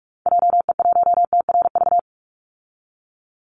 Morse code, heard as a keyed tone, about 44 wpm, 700 Hz. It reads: JE1TRV
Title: Call Signs Repeated 2x - 45wpm